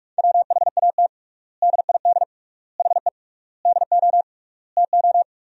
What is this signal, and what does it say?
Morse code, heard as a keyed tone, about 46 wpm, 700 Hz. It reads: WHAT DID HE DO TO
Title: 3x Word-Spacing - Sentences from Top 100 Words - 45wpm